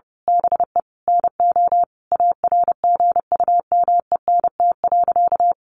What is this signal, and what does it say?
Morse code, heard as a keyed tone, about 30 wpm, 700 Hz. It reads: BE NO ARGUMENT.
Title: A Sound Of Thunder 30wpm